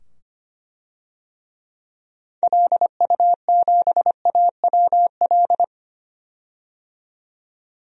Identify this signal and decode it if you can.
Morse code, heard as a keyed tone, about 25 wpm, 700 Hz. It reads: LU7AWL
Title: DXCC-100 Call Signs - 6 Characters - 25wpm